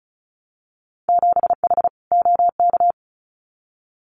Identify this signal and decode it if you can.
Morse code, heard as a keyed tone, about 34 wpm, 700 Hz. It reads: 7H OK
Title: Winter Field Day Exchanges Encore 35wpm